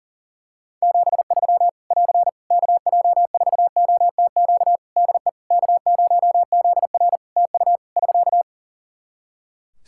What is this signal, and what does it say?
Morse code, heard as a keyed tone, about 40 wpm, 700 Hz. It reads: 73 <AR> KJ4OTQ DE K0ZR TU <SK>
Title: Practice QSO - Exchange 4 - Ending - 40wpm